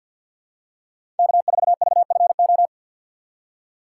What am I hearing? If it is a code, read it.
Morse code, heard as a keyed tone, about 52 wpm, 700 Hz. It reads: K4VFO
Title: USA Call Signs - 2x Character Spacing - 50wpm